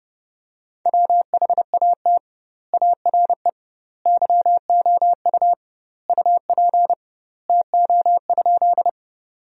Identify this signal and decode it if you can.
Morse code, heard as a keyed tone, about 29 wpm, 700 Hz. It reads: WHAT ARE YOU UP TO?
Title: 2x Word-Spacing - Sentences from Top 100 Words - 30wpm